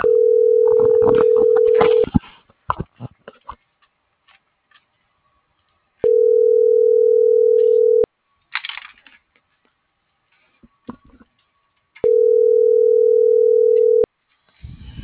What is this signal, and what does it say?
Ringback tone (North America), 440 + 480 Hz.